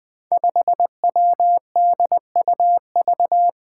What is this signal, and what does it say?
Morse code, heard as a keyed tone, about 20 wpm, 700 Hz. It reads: 5WDUV
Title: Five Challenging Letter-Number Combinations 20wpm